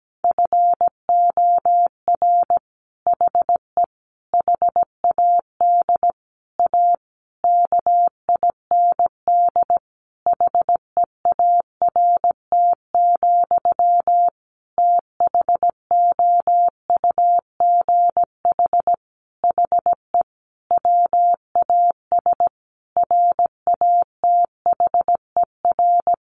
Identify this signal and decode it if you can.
Morse code, heard as a keyed tone, about 17 wpm, 700 Hz. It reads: FOR HE HAD A KIND HEART, THOUGH HE WAS RATHER
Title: Sentences from Top 1400 Words 17wpm